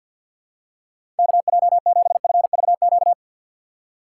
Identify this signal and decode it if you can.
Morse code, heard as a keyed tone, about 51 wpm, 700 Hz. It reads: KJ7FVQ
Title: USA Call Signs - 6x Character Spacing - 50wpm